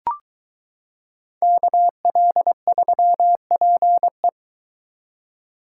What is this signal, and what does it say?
Morse code, heard as a keyed tone, about 22 wpm, 700 Hz. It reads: KL3PE